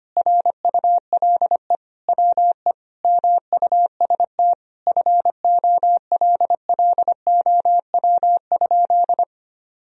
Morse code, heard as a keyed tone, about 25 wpm, 700 Hz. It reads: RULE WE MUST FOLLOW?
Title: Sentences from Top 1000 Words - Review Part 2 - 25wpm